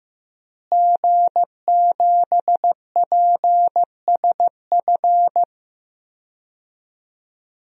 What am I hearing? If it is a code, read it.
Morse code, heard as a keyed tone, about 15 wpm, 700 Hz. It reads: G7PSF
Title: DXCC-100 Call Signs - 5 Characters - Repeated 3x - 15wpm